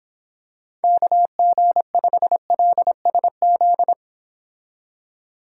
Morse code, heard as a keyed tone, about 26 wpm, 700 Hz. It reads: KG5LSZ